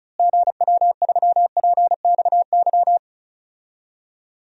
Morse code, heard as a keyed tone, about 36 wpm, 700 Hz. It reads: GW3PXY
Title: DXCC-100 Call Signs - 6 Characters - Repeated 3x - 35wpm